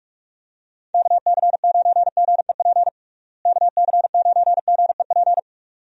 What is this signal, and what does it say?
Morse code, heard as a keyed tone, about 46 wpm, 700 Hz. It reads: KC9GEP KC9GEP
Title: USA Call Signs - 6 Characters - Repeated 2x - 45wpm